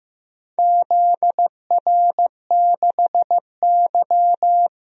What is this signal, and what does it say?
Morse code, heard as a keyed tone, about 15 wpm, 700 Hz. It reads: ZR6Y
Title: DXCC-100 Call Signs - 4 Characters - 15wpm